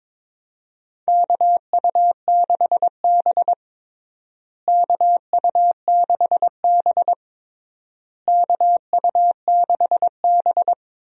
Morse code, heard as a keyed tone, about 22 wpm, 700 Hz. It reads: KU6B KU6B KU6B
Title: USA Call Signs - 4 Characters - Repeated 3x - 22wpm